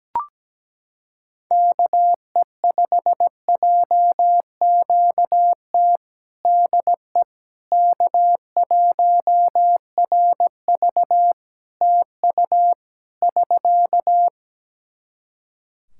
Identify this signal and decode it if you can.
Morse code, heard as a keyed tone, about 17 wpm, 700 Hz. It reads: KE5JQT DE K1RV TU <SK>
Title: Practice QSO - Exchange 4 - Ending - 17wpm